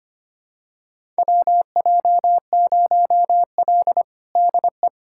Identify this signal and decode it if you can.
Morse code, heard as a keyed tone, about 25 wpm, 700 Hz. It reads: WJ0L DE
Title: USA Callsign DE Callsign - 4 Characters - 25wpm